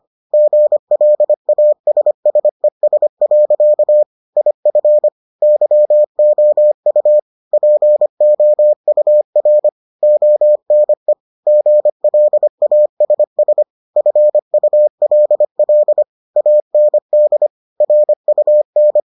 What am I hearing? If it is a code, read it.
Morse code, heard as a keyed tone, about 25 wpm, 600 Hz. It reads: GLASSES. IF YOU POUR ONE GLASS FULL AND RUN